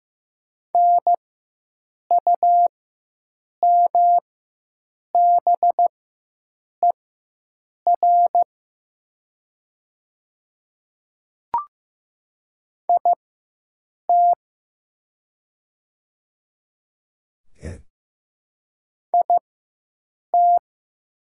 Morse code, heard as a keyed tone, about 15 wpm, spaced as for 7 wpm, 700 Hz. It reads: NUMBER IT IT
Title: 4x Character Spacing - Top 100 Words - 15wpm